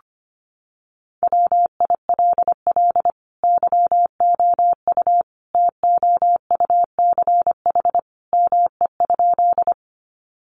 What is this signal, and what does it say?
Morse code, heard as a keyed tone, about 25 wpm, 700 Hz. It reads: WILL YOU TOUCH ME?
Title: Sentences from Top 1200 Words 25wpm